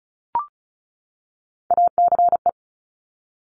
Morse code, heard as a keyed tone, about 36 wpm, 700 Hz. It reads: ACE
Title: Top Three Letter Combinations in Words 35wpm